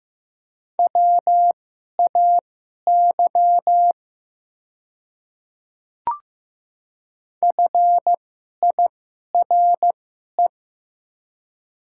Morse code, heard as a keyed tone, about 15 wpm, spaced as for 11 wpm, 700 Hz. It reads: WAY FIRE
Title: 2x Character Spacing - Top 500 Words - 15wpm